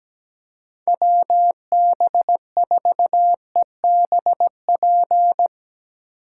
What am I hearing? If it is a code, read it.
Morse code, heard as a keyed tone, about 17 wpm, 700 Hz. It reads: WB4EBP
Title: USA Call Signs - 6 Characters - 4x Character Spacing - 17wpm